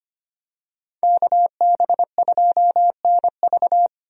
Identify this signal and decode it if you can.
Morse code, heard as a keyed tone, about 25 wpm, 700 Hz. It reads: KB2NV
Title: USA Call Signs - 5 Characters - Repeated 2x - 25wpm